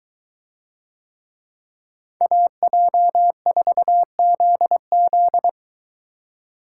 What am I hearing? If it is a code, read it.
Morse code, heard as a keyed tone, about 22 wpm, 700 Hz. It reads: AJ4ZZ